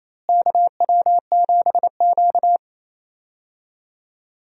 Morse code, heard as a keyed tone, about 28 wpm, 700 Hz. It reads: KW7Q
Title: USA Call Signs - 4 Characters - Repeated 2x - 28wpm